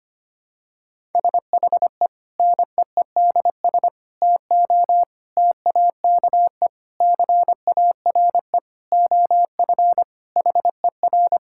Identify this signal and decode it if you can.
Morse code, heard as a keyed tone, about 25 wpm, 700 Hz. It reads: SHE NEEDS TO TAKE CARE OF HER
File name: practice / 25wpm / Sentences with 4-Word Phrases - 25wpm